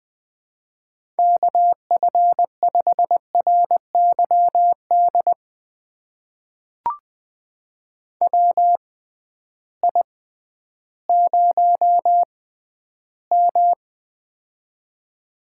Morse code, heard as a keyed tone, about 20 wpm, spaced as for 12 wpm, 700 Hz. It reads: KF5RYD WI0M
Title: USA Call Signs - 6x Character Spacing - 20wpm